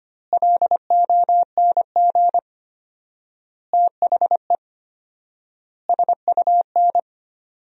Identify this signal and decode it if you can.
Morse code, heard as a keyed tone, about 25 wpm, 700 Hz. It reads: LONG THE SUN
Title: 4x Word-Spacing - Sentences from Top 400 Words - 25wpm